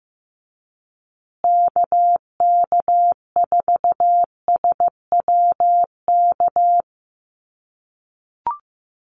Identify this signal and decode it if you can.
Morse code, heard as a keyed tone, about 15 wpm, 700 Hz. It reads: KK4SWK
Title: USA Call Signs Repeated 2x 15wpm